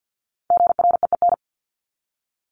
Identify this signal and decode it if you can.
Morse code, heard as a keyed tone, about 50 wpm, 700 Hz. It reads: GREEN
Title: Top 500 Words 50wpm